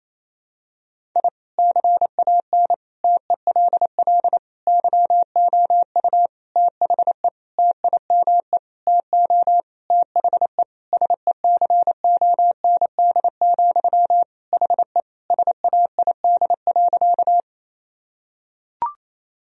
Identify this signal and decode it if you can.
Morse code, heard as a keyed tone, about 27 wpm, 700 Hz. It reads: I CAN TELL YOU THE TIME TO THE SECOND, HE SAID.
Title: Sentences from Top 1000 Words - Review Part 1 - 28wpm